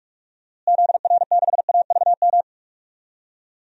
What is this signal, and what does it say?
Morse code, heard as a keyed tone, about 41 wpm, 700 Hz. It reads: ZR6AVM
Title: DXCC-100 Call Signs - 6 Characters - Repeated 3x - 45wpm